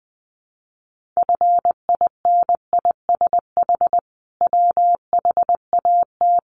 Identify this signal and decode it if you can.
Morse code, heard as a keyed tone, about 20 wpm, 700 Hz. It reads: FINISH WHAT
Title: Sentences from Top 1200 Words 20wpm